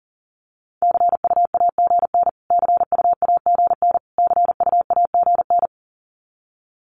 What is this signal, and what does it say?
Morse code, heard as a keyed tone, about 40 wpm, 700 Hz. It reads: CUAGN CUAGN CUAGN
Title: Abbreviations Repeated 3x - 40wpm